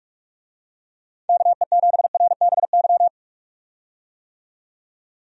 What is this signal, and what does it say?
Morse code, heard as a keyed tone, about 46 wpm, 700 Hz. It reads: KE7RBY
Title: USA Call Signs - 6 Characters - 45wpm